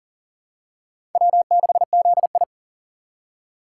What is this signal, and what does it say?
Morse code, heard as a keyed tone, about 40 wpm, 700 Hz. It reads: W6ZI
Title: USA Call Signs - 4 Characters - 3x Character Spacing - 40wpm